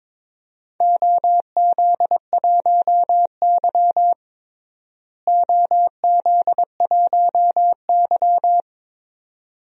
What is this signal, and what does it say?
Morse code, heard as a keyed tone, about 21 wpm, 700 Hz. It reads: OZ1Y OZ1Y
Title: DXCC-100 Call Signs - 4 Characters - Repeated 2x - 22wpm